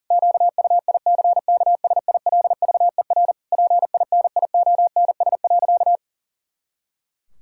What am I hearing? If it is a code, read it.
Morse code, heard as a keyed tone, about 40 wpm, 700 Hz. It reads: QUICKSILVER PINIONS.
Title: Pangram Challenge v2 - 40wpm